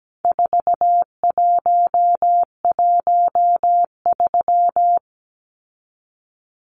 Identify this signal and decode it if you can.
Morse code, heard as a keyed tone, about 17 wpm, 700 Hz. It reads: 4113
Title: Numbers - 4-Digit 17wpm